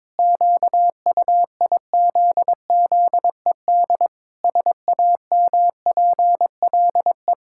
Morse code, heard as a keyed tone, about 22 wpm, 700 Hz. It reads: QUIZZED SAMPLE
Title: Pangram Challenge v2 - 22wpm